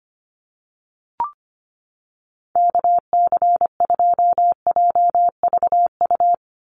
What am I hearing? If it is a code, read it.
Morse code, heard as a keyed tone, about 25 wpm, 700 Hz. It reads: KC2JVU
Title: USA Call Signs 25wpm